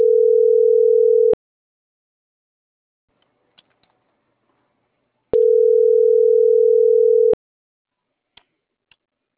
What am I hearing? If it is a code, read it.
Ringback tone (North America), 440 + 480 Hz.